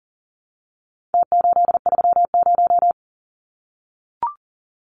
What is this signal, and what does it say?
Morse code, heard as a keyed tone, about 40 wpm, 700 Hz. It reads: T830
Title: Numbers - Contest Style - 4-Digit 40wpm